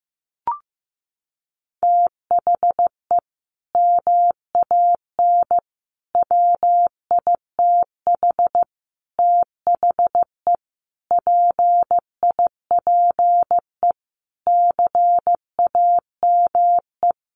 Morse code, heard as a keyed tone, about 15 wpm, 700 Hz. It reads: THE MAN WITH THE PIPE CAME
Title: Sentences from Top 1500 Words 15wpm